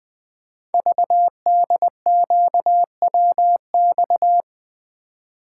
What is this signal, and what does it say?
Morse code, heard as a keyed tone, about 20 wpm, 700 Hz. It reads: VDQWX
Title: Code Groups - 5 Letter:Numbers - 20wpm